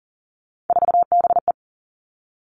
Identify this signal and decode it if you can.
Morse code, heard as a keyed tone, about 39 wpm, 700 Hz. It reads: 4BE
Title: Three Challenging Letter-Number Combinations 40wpm